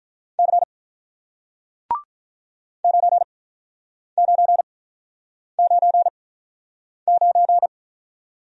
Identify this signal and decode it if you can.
Morse code, heard as a keyed tone, about 46 wpm, spaced as for 9 wpm, 700 Hz. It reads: C 9999